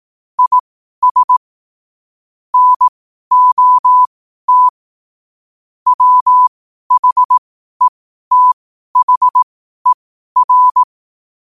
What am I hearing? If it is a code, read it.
Morse code, heard as a keyed tone, about 18 wpm, spaced as for 12 wpm, 1000 Hz. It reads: IS NOT WHETHER
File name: qotdfiles / 12wpm